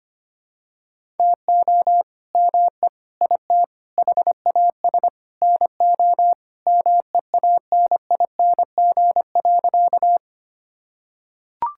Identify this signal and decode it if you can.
Morse code, heard as a keyed tone, about 25 wpm, 700 Hz. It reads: TO ME IT HAS NO MEANING.
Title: Sentences from Top 1000 Words - Review Part 6 - 25wpm